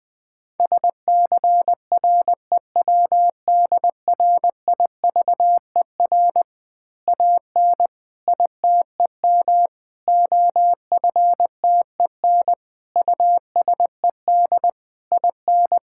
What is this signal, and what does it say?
Morse code, heard as a keyed tone, about 20 wpm, 700 Hz. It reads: SCREWDRIVER AN ITEM OFTEN USED IN
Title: Easy Words with Hints - 20wpm